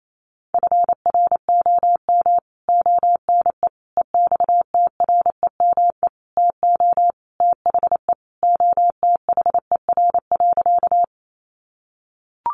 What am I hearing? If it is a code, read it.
Morse code, heard as a keyed tone, about 28 wpm, 700 Hz. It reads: FROM ONE EXTREME TO THE OTHER.
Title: Sentences from Top 1300 Words 28wpm